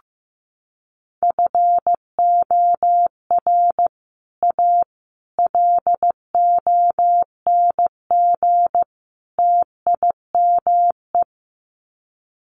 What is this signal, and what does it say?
Morse code, heard as a keyed tone, about 15 wpm, 700 Hz. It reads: FOR A LONG TIME
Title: Sets of 4 Words 15wpm